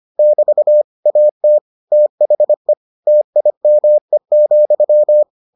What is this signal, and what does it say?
Morse code, heard as a keyed tone, about 25 wpm, 600 Hz. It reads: <BT> AT THE TIME,